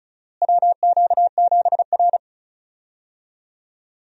Morse code, heard as a keyed tone, about 36 wpm, 700 Hz. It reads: WQ7R
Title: USA Call Signs - 4 Characters - Repeated 3x - 35wpm